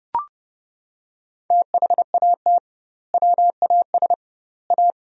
Morse code, heard as a keyed tone, about 30 wpm, 700 Hz. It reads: THAT WAS A
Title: 2x Word-Spacing - Sentences from Top 200 Words - 30wpm